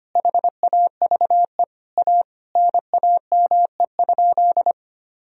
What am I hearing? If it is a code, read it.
Morse code, heard as a keyed tone, about 25 wpm, 700 Hz. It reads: HAVE A NAME?
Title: Sentences from Top 1000 Words - Review Part 3 - 25wpm